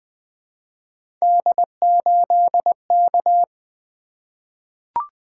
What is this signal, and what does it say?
Morse code, heard as a keyed tone, about 20 wpm, 700 Hz. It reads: D8K
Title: Three Letter-Number Combinations - 20wpm